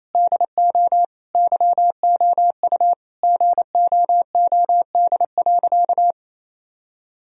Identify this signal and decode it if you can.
Morse code, heard as a keyed tone, about 28 wpm, 700 Hz. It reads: DO YOU GOOD.
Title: Sentences from Top 1000 Words - Review Part 1 - 28wpm